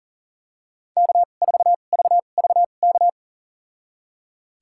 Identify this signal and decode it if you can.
Morse code, heard as a keyed tone, about 37 wpm, spaced as for 28 wpm, 700 Hz. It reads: K4VVK
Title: USA Call Signs - 5 Characters - 2x Character Spacing - 40wpm